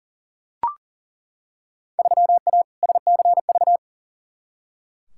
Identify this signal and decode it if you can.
Morse code, heard as a keyed tone, about 40 wpm, 700 Hz. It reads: 3A SCV
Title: Field Day Exchanges - 40wpm